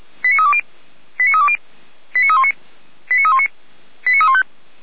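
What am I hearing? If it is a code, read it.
Five-tone selective call (ZVEI), heard as quick runs of five tones, tones of about 70 ms. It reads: 89320 89320 89319 89319 89315
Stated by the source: Five-tone ZVEI selective call received on an SDR.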